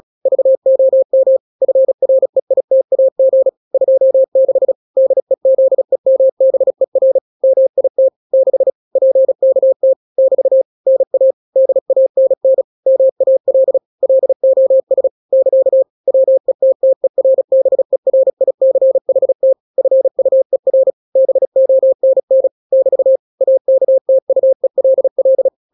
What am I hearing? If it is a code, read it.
Morse code, heard as a keyed tone, about 36 wpm, 530 Hz. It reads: VOM FREITAG 26 DEZEMBER MIT 6 PKT <BT> NA DANN MAL LOS <KA> WETTERBERICHT FUER BONN <BT> AKTUELL
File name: daily / 35wpm